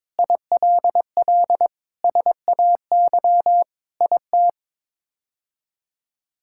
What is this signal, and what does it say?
Morse code, heard as a keyed tone, about 22 wpm, 700 Hz. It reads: ILL SAY IT
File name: practice / 22wpm / Sets of 8 Words - 22wpm